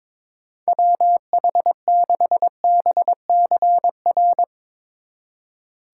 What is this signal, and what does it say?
Morse code, heard as a keyed tone, about 22 wpm, 700 Hz. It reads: WH6BCR
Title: Speed Racing - USA 2x3 Call Signs 22wpm